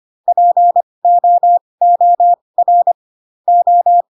Morse code, heard as a keyed tone, about 25 wpm, spaced as for 20 wpm, 700 Hz. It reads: POOR O
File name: Pinocchio / Pinocchio ch4 15 25 ik1hge